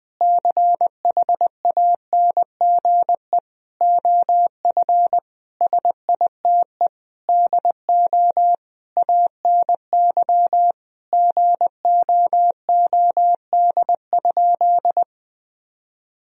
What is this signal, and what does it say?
Morse code, heard as a keyed tone, about 21 wpm, 700 Hz. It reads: CHANGE OF SITE DO ANY GOOD?
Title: Sentences from Top 1000 Words - Review Part 4 - 20wpm